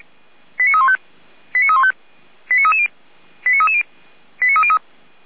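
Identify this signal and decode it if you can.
Five-tone selective call (ZVEI), heard as quick runs of five tones, tones of about 70 ms. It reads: 89315 89315 89330 89330 89303
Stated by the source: ZVEI five-tone call received by SDR.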